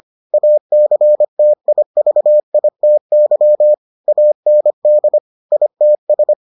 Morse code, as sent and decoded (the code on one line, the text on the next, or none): .- -.-. - .. ...- .. - -.-- / .- -. -.. / .. - ...
ACTIVITY AND ITS